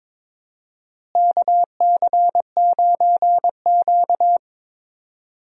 Morse code, heard as a keyed tone, about 22 wpm, 700 Hz. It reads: KC9Q
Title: USA Call Signs - 4 Characters - Repeated 3x - 22wpm